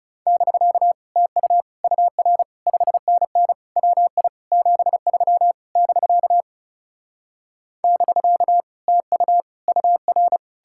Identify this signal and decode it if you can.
Morse code, heard as a keyed tone, about 40 wpm, 700 Hz. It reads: <BK> TU UR 5NN WI 73 <BK> <BK> TU UR